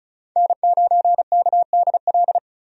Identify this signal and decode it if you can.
Morse code, heard as a keyed tone, about 36 wpm, 700 Hz. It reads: N9KDL
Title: USA Call Signs - 5 Characters - 35wpm